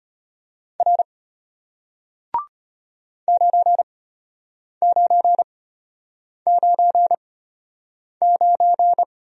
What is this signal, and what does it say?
Morse code, heard as a keyed tone, about 33 wpm, spaced as for 8 wpm, 700 Hz. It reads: R 9999